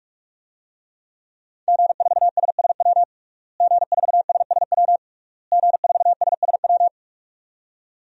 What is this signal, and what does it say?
Morse code, heard as a keyed tone, about 43 wpm, 700 Hz. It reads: G4SSW G4SSW G4SSW
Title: DXCC-100 Call Signs - 5 Characters - Repeated 3x - 45wpm